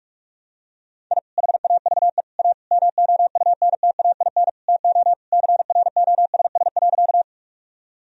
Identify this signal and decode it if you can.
Morse code, heard as a keyed tone, about 46 wpm, 700 Hz. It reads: I HAVE A MOUNTAIN TO CROSS.
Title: Sentences from Top 1000 Words - Review Part 1 - 45wpm